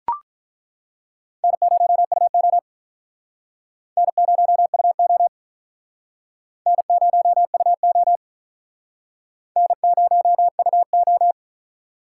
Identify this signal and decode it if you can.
Morse code, heard as a keyed tone, about 48 wpm, 700 Hz. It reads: N0UO N0UO N0UO N0UO